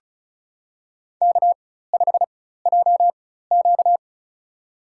Morse code, heard as a keyed tone, about 36 wpm, spaced as for 16 wpm, 700 Hz. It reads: K5JQ
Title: USA Call Signs - 4 Characters - 4x Character Spacing - 35wpm